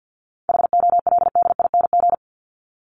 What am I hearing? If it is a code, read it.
Morse code, heard as a keyed tone, about 50 wpm, 700 Hz. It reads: HOLDING
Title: ING Suffix - Top 100 Words with Base Words 50wpm